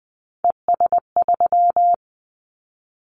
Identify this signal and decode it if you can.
Morse code, heard as a keyed tone, about 20 wpm, 700 Hz. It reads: ES3
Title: Callsign Prefixes for Top 50 DXCC - 20wpm